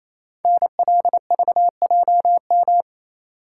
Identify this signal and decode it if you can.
Morse code, heard as a keyed tone, about 27 wpm, 690 Hz. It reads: NLVJM
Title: Code Groups - 5 Letters - 28wpm